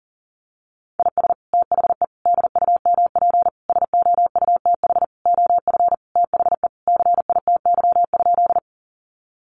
Morse code, heard as a keyed tone, about 40 wpm, 700 Hz. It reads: IS THE DUMP SOUTH OF THE CITY?
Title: Sentences from Top 1300 Words 40wpm